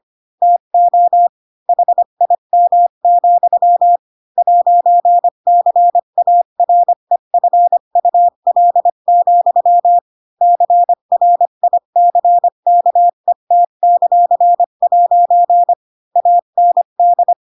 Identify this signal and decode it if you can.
Morse code, heard as a keyed tone, about 25 wpm, 700 Hz. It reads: TO HIM, 'CAREFUL, CRICKET;' AND
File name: Pinocchio / Pinocchio ch7 20 25 ik1hge